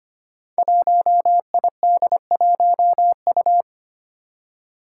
Morse code, heard as a keyed tone, about 25 wpm, 700 Hz. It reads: 1ID1U
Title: Five Challenging Letter-Number Combinations 25wpm